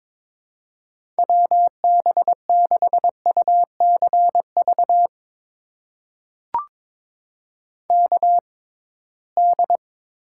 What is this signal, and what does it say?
Morse code, heard as a keyed tone, about 22 wpm, spaced as for 17 wpm, 700 Hz. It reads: WB6UCV KD
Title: USA Call Signs - 6 Characters - 6x Character Spacing - 22wpm